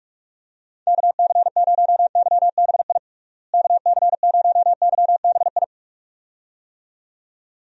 Morse code, heard as a keyed tone, about 46 wpm, 700 Hz. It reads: KC0YBI KC0YBI
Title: USA Call Signs - 6 Characters - Repeated 2x - 45wpm